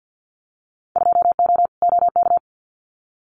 Morse code, heard as a keyed tone, about 50 wpm, 700 Hz. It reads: 2O OK
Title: Winter Field Day Exchanges Encore 50wpm